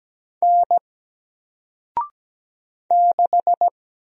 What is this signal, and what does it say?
Morse code, heard as a keyed tone, about 17 wpm, 700 Hz. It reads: N 6
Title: Single Letter-Number 17wpm